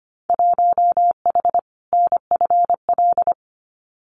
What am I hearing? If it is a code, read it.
Morse code, heard as a keyed tone, about 25 wpm, 700 Hz. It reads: 1H NFL
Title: Winter Field Day Exchanges 25wpm